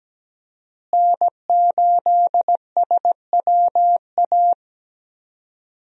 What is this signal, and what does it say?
Morse code, heard as a keyed tone, about 17 wpm, 700 Hz. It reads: N8SWA
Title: USA Call Signs - 5 Characters - Repeated 2x - 17wpm